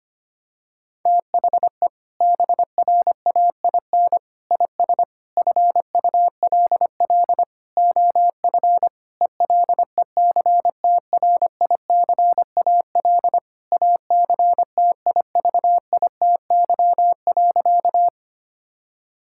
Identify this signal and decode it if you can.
Morse code, heard as a keyed tone, about 25 wpm, 700 Hz. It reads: THE BRAIN IS FULL OF ELECTRICAL ACTIVITY.
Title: Sentences from Top 1000 Words - Review Part 6 - 25wpm